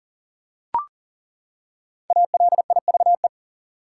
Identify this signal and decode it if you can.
Morse code, heard as a keyed tone, about 38 wpm, 700 Hz. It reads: ALIVE
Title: Five Letter Words - 40wpm